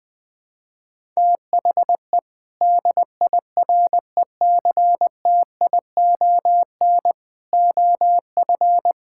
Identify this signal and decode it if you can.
Morse code, heard as a keyed tone, about 20 wpm, 700 Hz. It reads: THE DIRECTION OF
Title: Sentences from Top 1000 Words - Review Part 5 - 20wpm